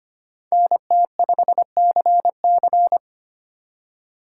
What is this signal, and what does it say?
Morse code, heard as a keyed tone, about 31 wpm, 700 Hz. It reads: NT5CC
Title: Speed Racing - USA 2x2 Call Signs 25wpm